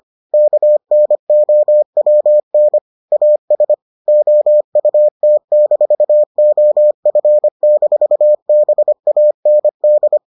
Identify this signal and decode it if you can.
Morse code, heard as a keyed tone, about 25 wpm, 600 Hz. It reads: KNOWN AS OUT-OF-BAND